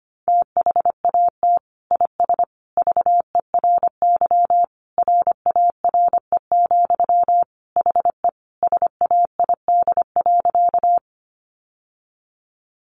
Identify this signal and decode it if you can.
Morse code, heard as a keyed tone, about 25 wpm, 700 Hz. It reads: THAT IS VERY RARE, HE SAID.
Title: Sentences from Top 1100 Words 25wpm